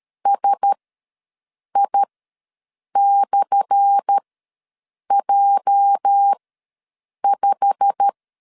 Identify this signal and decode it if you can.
Morse code, heard as a keyed tone, about 13 wpm, spaced as for 7 wpm, 800 Hz. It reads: SI/J5
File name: LEZIONI / 10 SIGNS / 10A SLASH ws 12wpm 3443